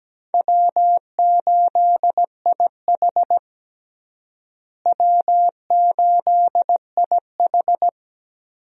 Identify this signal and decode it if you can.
Morse code, heard as a keyed tone, about 17 wpm, 700 Hz. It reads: W8IH W8IH
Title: USA Call Signs - 4 Characters - Repeated 2x - 17wpm